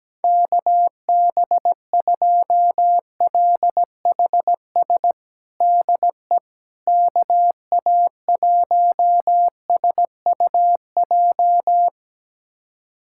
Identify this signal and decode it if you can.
Morse code, heard as a keyed tone, about 17 wpm, 700 Hz. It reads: KB2LHS DE KA1SUJ
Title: USA Callsign DE Callsign 17wpm